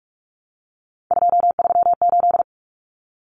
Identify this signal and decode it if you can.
Morse code, heard as a keyed tone, about 44 wpm, 700 Hz. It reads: 238
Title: Common Cut Numbers - 3-Digit 45wpm